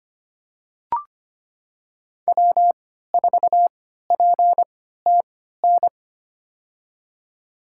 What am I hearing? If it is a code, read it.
Morse code, heard as a keyed tone, about 25 wpm, spaced as for 14 wpm, 700 Hz. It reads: W4PTN
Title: USA Call Signs - 5 Characters - 3x Character Spacing - 25wpm